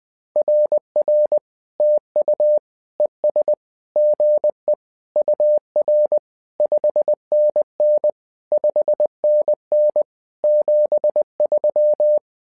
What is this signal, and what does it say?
Morse code, heard as a keyed tone, about 20 wpm, 600 Hz. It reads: RR TU ES GE UR 5NN 5NN 73
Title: SOTA QSO Practice 20wpm